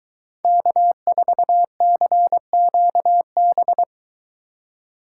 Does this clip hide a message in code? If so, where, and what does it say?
Yes: Morse code from 0.4 to 3.8 s, heard as a keyed tone, about 22 wpm, 700 Hz. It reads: K4CQB